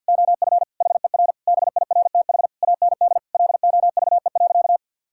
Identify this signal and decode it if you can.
Morse code, heard as a keyed tone, about 46 wpm, 700 Hz. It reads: OF HER BIRTH AND LOVE.
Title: Sentences from Top 1000 Words - Review Part 5 - 50wpm